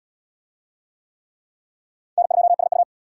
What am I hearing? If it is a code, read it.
Morse code, heard as a keyed tone, about 66 wpm, 700 Hz. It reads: N2SU